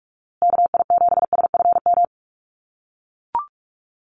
Morse code, heard as a keyed tone, about 44 wpm, 700 Hz. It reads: KI7SFM
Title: USA Call Signs Repeated 3x 45wpm